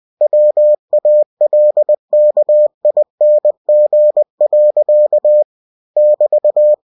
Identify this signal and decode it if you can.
Morse code, heard as a keyed tone, about 20 wpm, 600 Hz. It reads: WALKING. <BT>
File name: cw-books / winnie-the-pooh / winnie-20wpm-page-0021